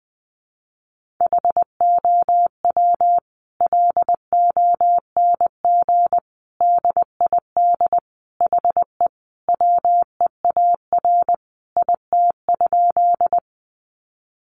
Morse code, heard as a keyed tone, about 20 wpm, 700 Hz. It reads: HOW LONG DID HE WEAR IT?
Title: Sentences from Top 1500 Words 20wpm